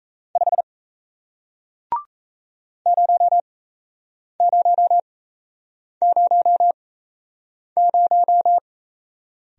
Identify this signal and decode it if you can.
Morse code, heard as a keyed tone, about 38 wpm, spaced as for 8 wpm, 700 Hz. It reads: 5 0000